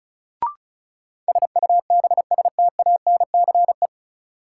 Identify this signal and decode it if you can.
Morse code, heard as a keyed tone, about 34 wpm, 700 Hz. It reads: SUBSTANCE
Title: Sentences from Top 1500 Words - New Word Drill Alt - 35wpm